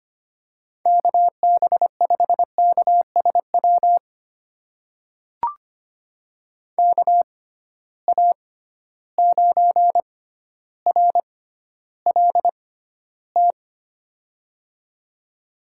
Morse code, heard as a keyed tone, about 25 wpm, spaced as for 13 wpm, 700 Hz. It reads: KB5KSW KA9RLT
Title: USA Call Signs - 6 Characters - 6x Character Spacing - 25wpm